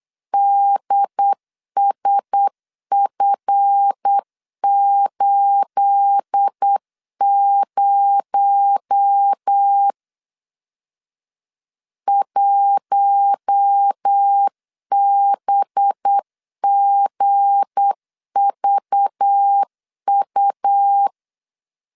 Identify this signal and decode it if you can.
Morse code, heard as a keyed tone, about 8 wpm, 800 Hz. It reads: DSF80 1BGVU
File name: CW / DET / 1 9 1500C S A 8wpm 5637